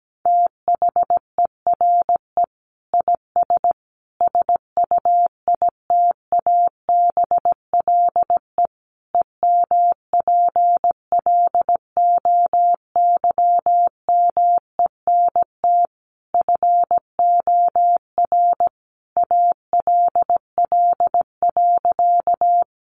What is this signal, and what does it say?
Morse code, heard as a keyed tone, about 17 wpm, 700 Hz. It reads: THERE IS SUITABLE EMPLOYMENT FOR ALL.
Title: Sentences from Top 1500 Words 17wpm